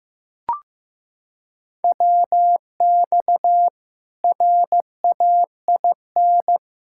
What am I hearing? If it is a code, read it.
Morse code, heard as a keyed tone, about 15 wpm, 700 Hz. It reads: WX RAIN
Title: Practice QSO - Exchange 2 - Info - 15wpm